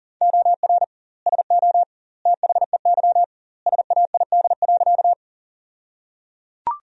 Morse code, heard as a keyed tone, about 37 wpm, 700 Hz. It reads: OR SO THEY SAID.
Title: 2x Word-Spacing - Sentences from Top 100 Words - 40wpm